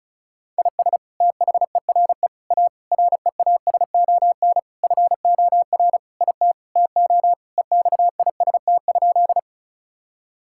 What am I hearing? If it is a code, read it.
Morse code, heard as a keyed tone, about 34 wpm, 700 Hz. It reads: IS THERE A REASON FOR IT TO EXIST?
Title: Sentences from Top 1000 Words - Review Part 5 - 35wpm